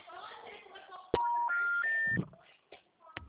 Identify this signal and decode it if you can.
Special information tone, with speech.